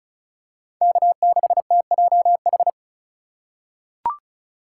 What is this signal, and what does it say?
Morse code, heard as a keyed tone, about 34 wpm, 700 Hz. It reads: K6TJH
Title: USA Call Signs - 5 Characters - 4x Character Spacing - 35wpm